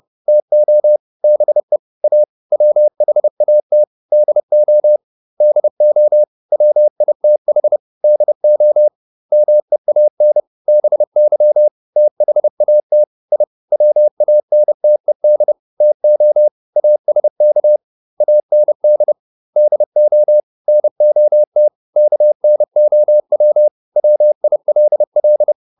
Morse code, heard as a keyed tone, about 30 wpm, 600 Hz. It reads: TO BE A WHAT DO DO WITH DO MEAN BY THAT I WANTED TO ASK AND DO NOT KNOW WILL